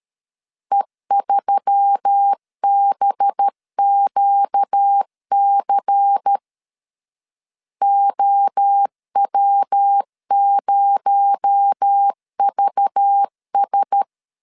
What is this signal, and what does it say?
Morse code, heard as a keyed tone, about 13 wpm, 800 Hz. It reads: E3BQC OW0VS
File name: LEZIONI / 6 PKX0 / 6 PKX0 12wpm 2231